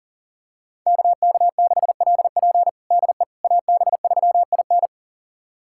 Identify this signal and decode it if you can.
Morse code, heard as a keyed tone, about 40 wpm, 700 Hz. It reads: KK6LP DE AB3IN
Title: USA Callsign DE Callsign - 5 Characters - 40wpm